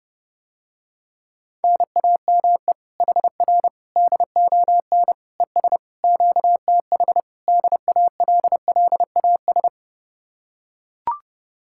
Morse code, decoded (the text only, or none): NAME HR DON ES QTH DALLAS